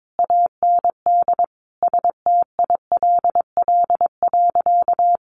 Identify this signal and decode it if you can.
Morse code, heard as a keyed tone, about 22 wpm, 700 Hz. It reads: AND STILL.
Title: Sentences from Top 1500 Words 22wpm